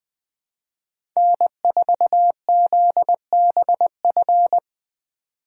Morse code, heard as a keyed tone, about 20 wpm, 700 Hz. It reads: N4ZBF
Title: Speed Racing - USA 1x3 Call Signs 17wpm